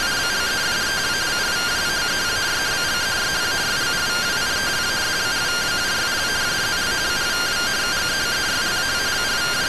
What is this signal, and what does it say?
Two-tone teleprinter signal, about 170 Hz shift.